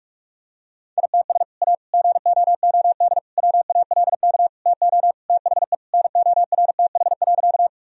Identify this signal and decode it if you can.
Morse code, heard as a keyed tone, about 46 wpm, 700 Hz. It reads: ITS A GOOD WALK TO THE NORTH.
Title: Sentences from Top 1000 Words - Review Part 1 - 45wpm